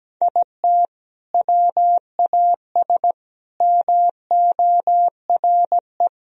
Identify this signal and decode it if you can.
Morse code, heard as a keyed tone, about 17 wpm, 700 Hz. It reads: IT WAS MORE
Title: Sentences from Top 1000 Words - Review Part 3 - 17wpm